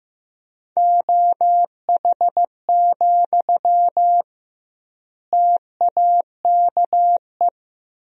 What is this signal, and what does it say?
Morse code, heard as a keyed tone, about 15 wpm, 700 Hz. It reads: OH, TAKE
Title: 2x Word-Spacing - Sentences from Top 500 Words - 15wpm